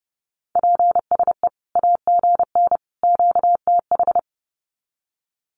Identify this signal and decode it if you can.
Morse code, heard as a keyed tone, about 30 wpm, 700 Hz. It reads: PSE AGN QTH
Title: Common QSO Phrases 30wpm